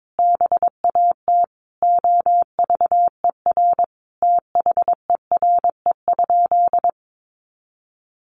Morse code, heard as a keyed tone, about 22 wpm, 700 Hz. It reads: BAT OVER THERE?
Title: Sentences from Top 1500 Words 22wpm